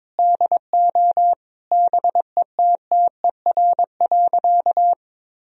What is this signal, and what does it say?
Morse code, heard as a keyed tone, about 22 wpm, 700 Hz. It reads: DO BETTER.
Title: Sentences from Top 1000 Words - Review Part 4 - 22wpm